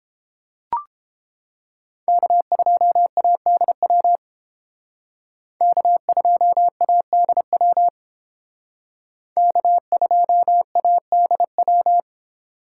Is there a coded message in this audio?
Yes: Morse code from 2.1 to 12.0 s, heard as a keyed tone, about 28 wpm, 700 Hz. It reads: K2ADW K2ADW K2ADW